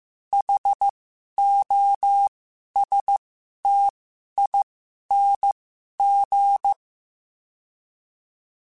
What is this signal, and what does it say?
Morse code, heard as a keyed tone, about 15 wpm, spaced as for 11 wpm, 800 Hz. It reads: HOSTING
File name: Special INGs-mp3s / Special INGs-mp3s / Sngl Ing @10